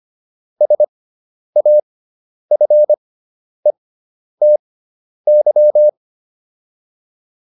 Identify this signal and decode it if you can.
Morse code, heard as a keyed tone, about 25 wpm, spaced as for 10 wpm, 600 Hz. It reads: SAFETY